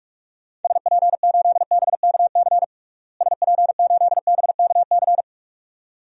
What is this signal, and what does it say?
Morse code, heard as a keyed tone, about 43 wpm, 700 Hz. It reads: SP8BKC SP8BKC
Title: DXCC-100 Call Signs - 6 Characters - Repeated 2x - 45wpm